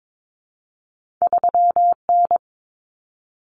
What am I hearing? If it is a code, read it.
Morse code, heard as a keyed tone, about 22 wpm, 700 Hz. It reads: 3N
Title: Common Cut Numbers - 2-Digit 22wpm